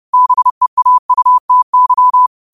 Morse code, heard as a keyed tone, about 30 wpm, 1000 Hz. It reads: BEAUTY
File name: qotdfiles / 30wpm